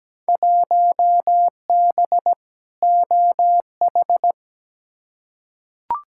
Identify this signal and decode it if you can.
Morse code, heard as a keyed tone, about 17 wpm, 700 Hz. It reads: 1B OH
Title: Field Day Exchanges - 17wpm